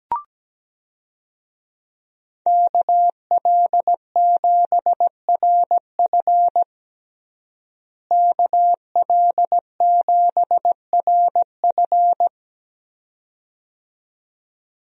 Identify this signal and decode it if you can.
Morse code, heard as a keyed tone, about 17 wpm, 700 Hz. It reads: KL7RF KL7RF
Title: DXCC-100 Call Signs - 5 Characters - Repeated 2x - 17wpm